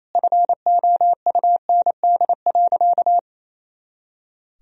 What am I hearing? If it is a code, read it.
Morse code, heard as a keyed tone, about 28 wpm, 700 Hz. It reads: FOUND.
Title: Sentences from Top 300 Words and Abbreviations 28wpm